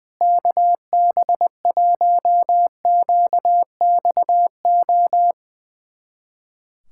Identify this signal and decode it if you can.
Morse code, heard as a keyed tone, about 21 wpm, 700 Hz. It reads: KB1QXO
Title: USA Callsign DE Callsign - 6 Characters - 20wpm